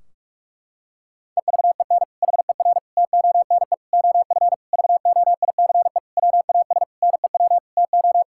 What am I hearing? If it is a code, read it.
Morse code, heard as a keyed tone, about 46 wpm, 700 Hz. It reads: EVEN HER TONE OF VOICE WAS NEW TO
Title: Sentences from Top 1600 Words 45wpm